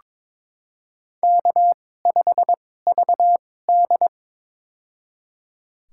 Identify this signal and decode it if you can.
Morse code, heard as a keyed tone, about 22 wpm, spaced as for 16 wpm, 700 Hz. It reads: K5VD
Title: USA Call Signs - 2x Character Spacing - 22wpm